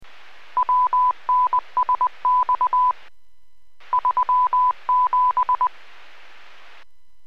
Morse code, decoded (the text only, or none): WNSX 37